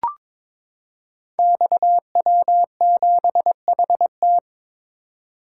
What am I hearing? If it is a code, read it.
Morse code, heard as a keyed tone, about 22 wpm, 700 Hz. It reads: XW7HT
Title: Code Groups - 5 Letter:Numbers - 22wpm